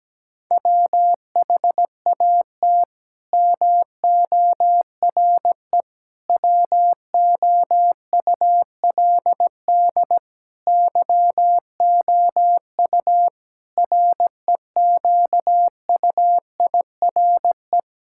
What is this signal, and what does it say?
Morse code, heard as a keyed tone, about 17 wpm, 700 Hz. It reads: WHAT MORE WOULD YOU REQUIRE
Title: Sentences from Top 1000 Words - Review Part 6 - 17wpm